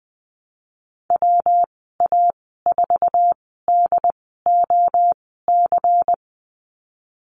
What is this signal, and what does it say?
Morse code, heard as a keyed tone, about 20 wpm, spaced as for 15 wpm, 700 Hz. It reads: WA4DOC
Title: 179 Lesson 29g - Slash - Call Signs 20wpm